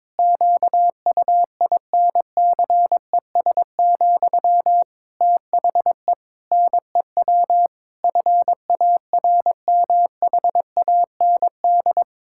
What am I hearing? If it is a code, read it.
Morse code, heard as a keyed tone, about 22 wpm, 700 Hz. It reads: QUINCES, THE NEW FARMHAND
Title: Pangram Challenge v2 - 22wpm